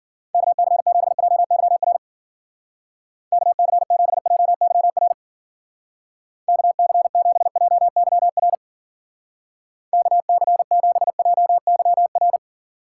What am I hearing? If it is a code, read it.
Morse code, heard as a keyed tone, about 52 wpm, 700 Hz. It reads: KC7JYR KC7JYR KC7JYR KC7JYR